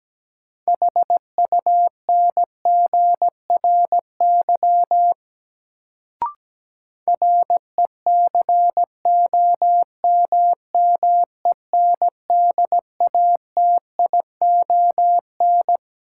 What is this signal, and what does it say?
Morse code, heard as a keyed tone, about 17 wpm, 700 Hz. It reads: HUNGRY RECOMMENDATION
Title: Sentences from Top 1400 Words - New Word Drill Alt - 17wpm